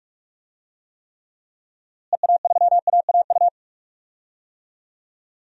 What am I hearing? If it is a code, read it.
Morse code, heard as a keyed tone, about 46 wpm, 700 Hz. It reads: EA3AAU
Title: DXCC-100 Call Signs - 6 Characters - Repeated 3x - 45wpm